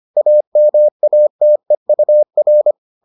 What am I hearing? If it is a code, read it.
Morse code, heard as a keyed tone, about 25 wpm, 600 Hz. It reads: AMATEUR